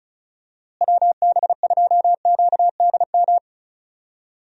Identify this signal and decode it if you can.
Morse code, heard as a keyed tone, about 34 wpm, 700 Hz. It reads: WB2QDM
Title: USA Call Signs - 6 Characters - Repeated 3x - 35wpm